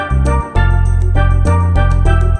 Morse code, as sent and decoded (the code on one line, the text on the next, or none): none